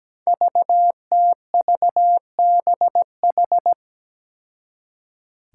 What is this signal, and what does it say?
Morse code, heard as a keyed tone, about 17 wpm, 700 Hz. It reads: VTVBH
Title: Five Challenging Letter-Number Combinations 17wpm